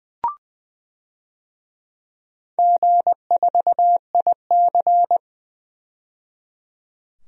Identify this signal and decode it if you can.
Morse code, heard as a keyed tone, about 20 wpm, 700 Hz. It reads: G4IC
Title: DXCC-100 Call Signs - 4 Characters - 20wpm